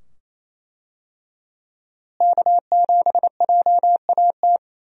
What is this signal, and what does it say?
Morse code, heard as a keyed tone, about 27 wpm, 700 Hz. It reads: K7JAT
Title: USA Call Signs - 5 Characters - Repeated 2x - 28wpm